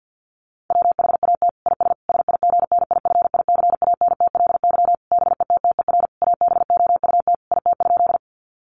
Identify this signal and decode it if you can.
Morse code, heard as a keyed tone, about 48 wpm, 700 Hz. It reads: WHAT IS SIGNIFICANTLY BETTER ABOUT IT?
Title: Sentences from Top 1500 Words 50wpm